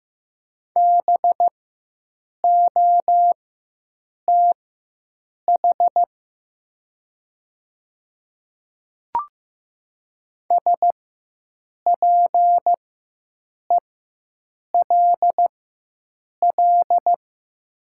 Morse code, heard as a keyed tone, about 15 wpm, spaced as for 7 wpm, 700 Hz. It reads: BOTH SPELL
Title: 4x Character Spacing - Top 300 Words - 15wpm